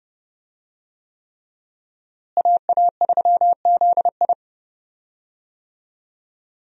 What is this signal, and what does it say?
Morse code, heard as a keyed tone, about 31 wpm, 700 Hz. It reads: AA3ZI
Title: USA Call Signs - 5 Characters - Repeated 3x - 30wpm